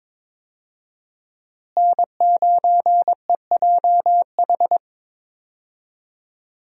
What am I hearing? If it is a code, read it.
Morse code, heard as a keyed tone, about 22 wpm, 700 Hz. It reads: N9EJH
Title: USA Call Signs - 5 Characters - Repeated 3x - 22wpm